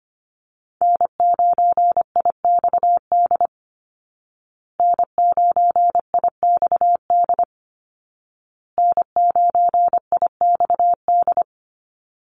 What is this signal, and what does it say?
Morse code, heard as a keyed tone, about 25 wpm, 700 Hz. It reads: N9IXD N9IXD N9IXD
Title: Call Signs Repeated 2x - 25wpm